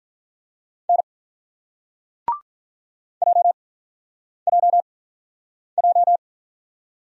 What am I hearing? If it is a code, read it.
Morse code, heard as a keyed tone, about 46 wpm, 700 Hz. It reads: N J J J